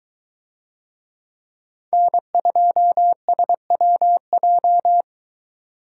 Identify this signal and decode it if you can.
Morse code, heard as a keyed tone, about 21 wpm, 700 Hz. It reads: N2SWJ